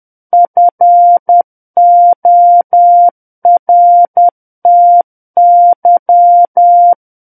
Morse code, heard as a keyed tone, about 10 wpm, 700 Hz. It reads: FORTY
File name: Book One / 01-Chapter 1 - 10 WPM